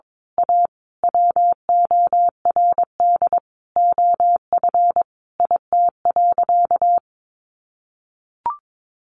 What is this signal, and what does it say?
Morse code, heard as a keyed tone, about 22 wpm, 700 Hz. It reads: A WORD OF IT.
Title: Sentences from Top 1200 Words 22wpm